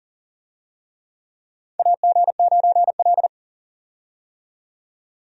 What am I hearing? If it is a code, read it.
Morse code, heard as a keyed tone, about 40 wpm, 700 Hz. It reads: AG9L
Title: USA Call Signs - 4 Characters - 40wpm